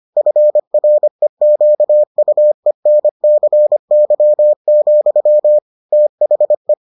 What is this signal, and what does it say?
Morse code, heard as a keyed tone, about 25 wpm, 600 Hz. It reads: FREQUENCY, THE